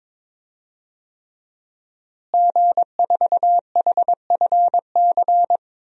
Morse code, heard as a keyed tone, about 22 wpm, 700 Hz. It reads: G4HFC
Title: DXCC-100 Call Signs - 5 Characters - Repeated 3x - 22wpm